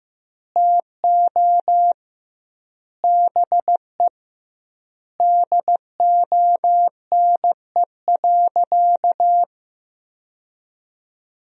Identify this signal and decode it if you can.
Morse code, heard as a keyed tone, about 15 wpm, 700 Hz. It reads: TO BE DONE.
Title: 2x Word-Spacing - Sentences from Top 500 Words - 15wpm